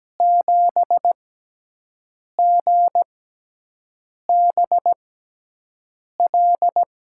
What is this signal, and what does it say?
Morse code, heard as a keyed tone, about 17 wpm, spaced as for 6 wpm, 700 Hz. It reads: 7GBL
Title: USA Call Signs - 6 Characters - 6x Character Spacing - 17wpm